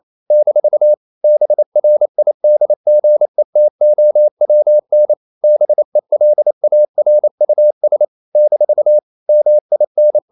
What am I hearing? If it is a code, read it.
Morse code, heard as a keyed tone, about 28 wpm, 600 Hz. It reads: - BRIDGETOWN BELARUS - MIN
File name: CW / Practice / NationalCapitals28